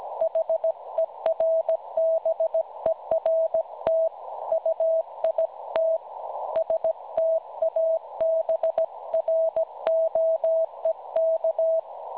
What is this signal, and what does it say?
Morse code, heard as a keyed tone, about 17 wpm, 650 Hz. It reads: HERBERT UIT STABROEK
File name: cwronde / websdr recording .6kHz